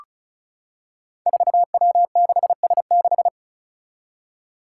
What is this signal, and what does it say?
Morse code, heard as a keyed tone, about 36 wpm, 700 Hz. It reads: 4W6S6
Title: Five Challenging Letter-Number Combinations 35wpm